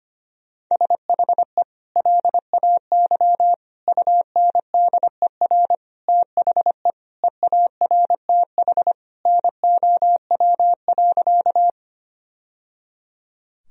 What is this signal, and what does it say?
Morse code, heard as a keyed tone, about 25 wpm, 700 Hz. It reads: SHE LAY UNDER THE EARTH NOW.
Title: Sentences from Top 1000 Words - Review Part 2 - 25wpm